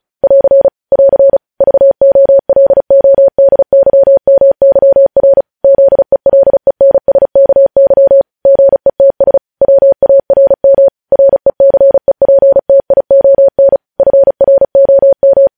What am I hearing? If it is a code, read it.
Morse code, heard as a keyed tone, about 36 wpm, 550 Hz. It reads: <AR> <AR> VOLODYMYR ZELENSKY GETS WARM RECEPTION FROM
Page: News Headlines in Morse Code at 35 WPM